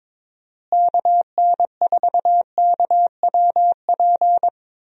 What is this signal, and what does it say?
Morse code, heard as a keyed tone, about 22 wpm, 700 Hz. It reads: KN4KWP
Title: USA Call Signs - 6 Characters - Repeated 2x - 22wpm